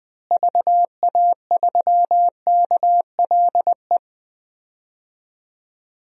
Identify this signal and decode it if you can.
Morse code, heard as a keyed tone, about 20 wpm, 700 Hz. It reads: VA3KLE
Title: DXCC-100 Call Signs - 6 Characters - Repeated 3x - 20wpm